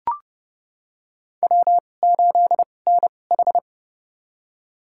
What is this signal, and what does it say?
Morse code, heard as a keyed tone, about 28 wpm, spaced as for 21 wpm, 700 Hz. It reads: W8NH
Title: USA Call Signs - 4 Characters - 2x Character Spacing - 30wpm